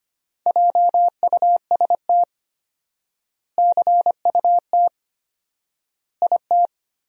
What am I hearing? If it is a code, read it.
Morse code, heard as a keyed tone, about 25 wpm, 700 Hz. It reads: JUST CUT IT
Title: 4x Word-Spacing - Sentences from Top 300 Words - 25wpm